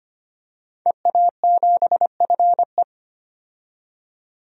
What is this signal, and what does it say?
Morse code, heard as a keyed tone, about 25 wpm, 700 Hz. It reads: EA7FE
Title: DXCC-100 Call Signs - 5 Characters - 25wpm